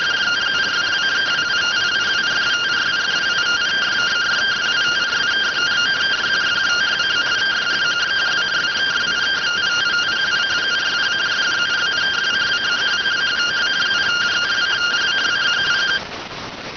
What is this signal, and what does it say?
Two-tone teleprinter signal, about 170 Hz shift.